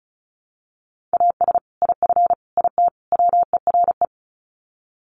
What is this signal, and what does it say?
Morse code, heard as a keyed tone, about 34 wpm, 700 Hz. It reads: AS IF IT WERE
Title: Sets of 4 Words 35wpm